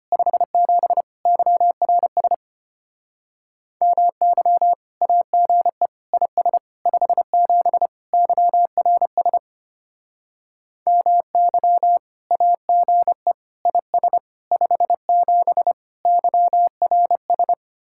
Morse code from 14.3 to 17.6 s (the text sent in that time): 57 YRS